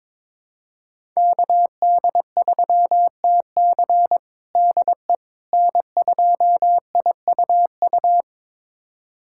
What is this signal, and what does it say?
Morse code, heard as a keyed tone, about 22 wpm, 700 Hz. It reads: KD3TC DE N2IUU
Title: USA Callsign DE Callsign - 5 Characters - 22wpm